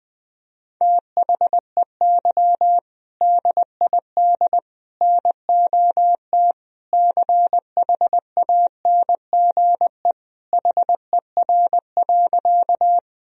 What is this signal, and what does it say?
Morse code, heard as a keyed tone, about 20 wpm, 700 Hz. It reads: THEY DID NOT CHANGE HER.
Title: Sentences from Top 1000 Words - Review Part 1 - 20wpm